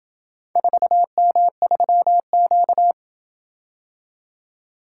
Morse code, heard as a keyed tone, about 27 wpm, 700 Hz. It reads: 4M3Q